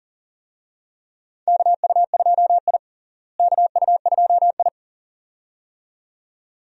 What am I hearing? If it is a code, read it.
Morse code, heard as a keyed tone, about 40 wpm, 700 Hz. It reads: KU2I KU2I
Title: USA Call Signs - 4 Characters - Repeated 2x - 40wpm